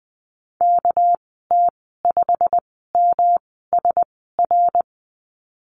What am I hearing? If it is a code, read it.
Morse code, heard as a keyed tone, about 20 wpm, spaced as for 15 wpm, 700 Hz. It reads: KT5MSR
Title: 231 Lesson 37g - Letter x - Call Signs 20wpm